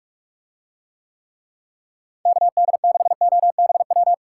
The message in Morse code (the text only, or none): KD6OBW